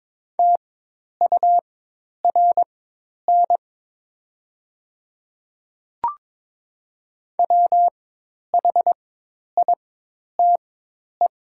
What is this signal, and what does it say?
Morse code, heard as a keyed tone, about 22 wpm, spaced as for 10 wpm, 700 Hz. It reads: TURN WHITE
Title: 4x Character Spacing - Top 500 Words - 22wpm